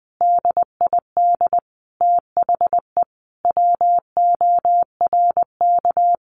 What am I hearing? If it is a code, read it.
Morse code, heard as a keyed tone, about 20 wpm, 700 Hz. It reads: DID THE WORK
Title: Sentences from Top 1200 Words 20wpm